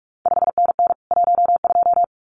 Morse code, heard as a keyed tone, about 46 wpm, 700 Hz. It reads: 5NN 12
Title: CQ WW DX Contest Exchange Encore - 45wpm